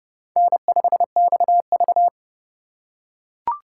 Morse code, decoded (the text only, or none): N5XV